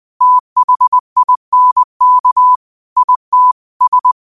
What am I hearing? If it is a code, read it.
Morse code, heard as a keyed tone, about 20 wpm, 1000 Hz. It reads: THINK IT S